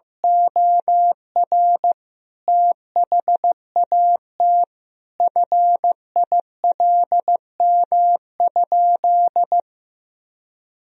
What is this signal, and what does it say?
Morse code, heard as a keyed tone, about 15 wpm, 700 Hz. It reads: OR THAT FILM?
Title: Sentences from Top 1000 Words - Review Part 3 - 15wpm